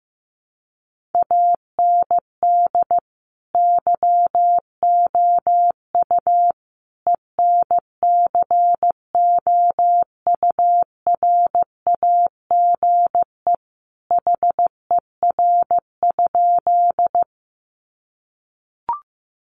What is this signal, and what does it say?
Morse code, heard as a keyed tone, about 15 wpm, 700 Hz. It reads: AND YOU ENCOURAGE HER?
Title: Sentences from Top 1400 Words 15wpm